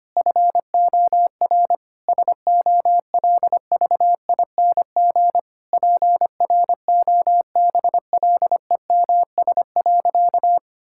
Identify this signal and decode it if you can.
Morse code, heard as a keyed tone, about 25 wpm, 700 Hz. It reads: FOR SOLVING PROBLEMS.
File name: practice / 25wpm / Difficult Sentences with Hints - 25wpm